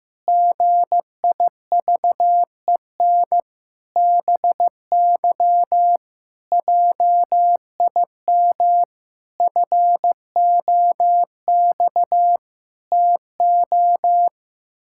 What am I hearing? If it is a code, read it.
Morse code, heard as a keyed tone, about 15 wpm, 700 Hz. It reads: GIVEN BY JIM FOX TO
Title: Pangram Challenge v2 - 15wpm